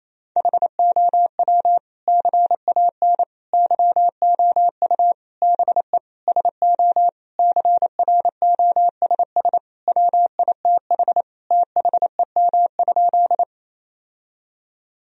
Morse code, heard as a keyed tone, about 28 wpm, 700 Hz. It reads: HOW CAN YOU BE SO CROSS WITH THEM?
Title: Sentences from Top 1000 Words - Review Part 1 - 28wpm